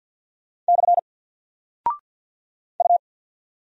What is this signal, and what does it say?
Morse code, heard as a keyed tone, about 48 wpm, 700 Hz. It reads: / U
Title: Single Character (Letter:Number:Punctuation:Prosign) 50wpm